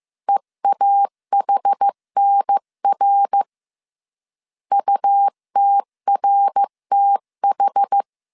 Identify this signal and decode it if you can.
Morse code, heard as a keyed tone, about 15 wpm, 800 Hz. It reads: EAHNR UTRTH
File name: CW / DET / 1 3 1500C S A 14wpm 2447